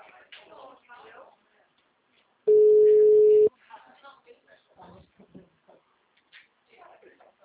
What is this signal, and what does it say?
Ringback tone (Europe, 425 Hz), with speech.
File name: out- -2002- - - .22493